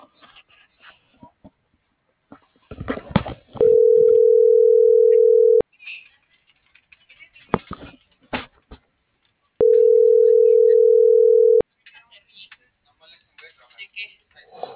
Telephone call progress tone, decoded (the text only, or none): ringback tone